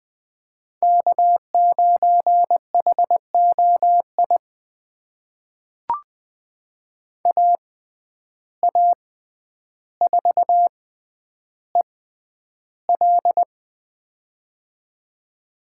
Morse code, heard as a keyed tone, about 20 wpm, spaced as for 10 wpm, 700 Hz. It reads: K9HOI AA4EL
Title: USA Call Signs - 5 Characters - 6x Character Spacing - 20wpm